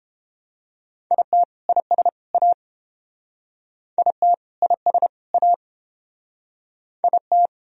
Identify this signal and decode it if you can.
Morse code, heard as a keyed tone, about 29 wpm, 700 Hz. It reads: IT IS A IT IS A IT